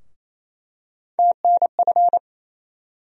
Morse code, heard as a keyed tone, about 28 wpm, 700 Hz. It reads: TNF
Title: USA Call Sign Postfixes - 3 Letters 28wpm